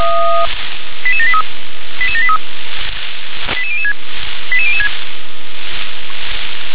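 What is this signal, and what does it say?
Five-tone selective call (ZVEI), heard as quick runs of five tones, tones of about 70 ms. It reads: 88783 88783 88006 88006
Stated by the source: ZVEI five-tone selective call received on an SDR.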